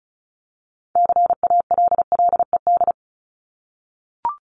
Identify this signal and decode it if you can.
Morse code, heard as a keyed tone, about 34 wpm, 700 Hz. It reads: CALLED
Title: ED Suffix - Top 100 Words with Base Words 35wpm